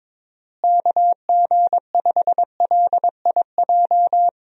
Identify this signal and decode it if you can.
Morse code, heard as a keyed tone, about 22 wpm, 700 Hz. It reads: KG5LIJ
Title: USA Call Signs - 6 Characters - Repeated 3x - 22wpm